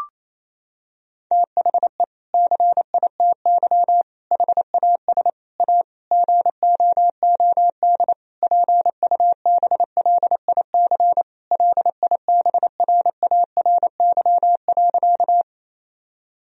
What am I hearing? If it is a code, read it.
Morse code, heard as a keyed tone, about 28 wpm, 700 Hz. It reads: THE CITY HAS A GOOD PUBLIC LIBRARY.
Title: Sentences from Top 1000 Words - Review Part 4 - 28wpm